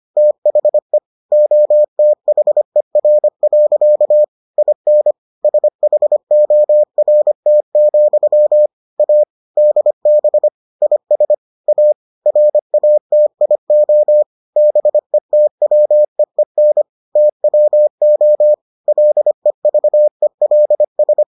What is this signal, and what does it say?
Morse code, heard as a keyed tone, about 25 wpm, 600 Hz. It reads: THE OTHER. IN SHORT, A DB IS A RATIO BETWEEN TWO LEVELS